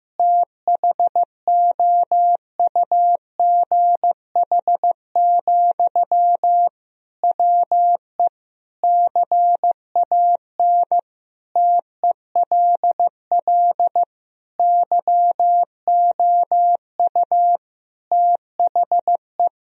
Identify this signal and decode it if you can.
Morse code, heard as a keyed tone, about 15 wpm, 700 Hz. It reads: THOUGH, WE CAN TELL YOU THE
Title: Sentences from Top 1000 Words - Review Part 2 - 15wpm